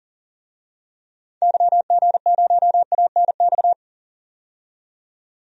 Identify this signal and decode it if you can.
Morse code, heard as a keyed tone, about 40 wpm, 700 Hz. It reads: YG0ANX
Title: DXCC-100 Call Signs - 6 Characters - 40wpm